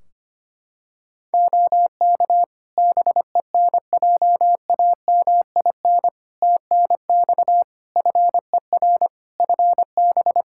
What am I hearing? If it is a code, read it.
Morse code, heard as a keyed tone, about 25 wpm, 700 Hz. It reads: OK BENJAMIN TNX FER FB
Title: Practice QSO - Exchange 4 - Ending - 25wpm